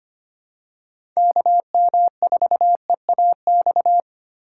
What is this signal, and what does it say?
Morse code, heard as a keyed tone, about 25 wpm, 700 Hz. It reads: KM4EAX
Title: USA Call Signs - 6 Characters - 4x Character Spacing - 25wpm